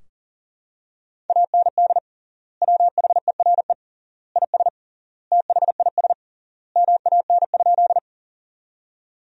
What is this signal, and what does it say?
Morse code, heard as a keyed tone, about 40 wpm, 700 Hz. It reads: AND WHERE IS THIS MAN?
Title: 3x Word-Spacing - Sentences from Top 200 Words - 40wpm